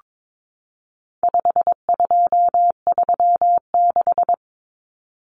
Morse code, heard as a keyed tone, about 22 wpm, 700 Hz. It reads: 5236
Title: Numbers - 4-Digit 22wpm